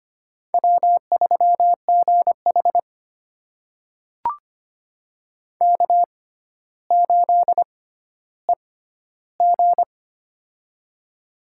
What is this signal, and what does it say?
Morse code, heard as a keyed tone, about 25 wpm, spaced as for 13 wpm, 700 Hz. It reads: W3GH K8EG
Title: USA Call Signs - 4 Characters - 6x Character Spacing - 25wpm